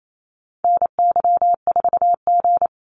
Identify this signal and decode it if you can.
Morse code, heard as a keyed tone, about 28 wpm, 700 Hz. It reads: NY4G
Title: NA SOTA Activator Callsigns 28wpm